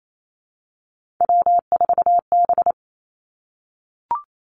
Morse code, heard as a keyed tone, about 28 wpm, 700 Hz. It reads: W4B
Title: Three Challenging Letter-Number Combinations 28wpm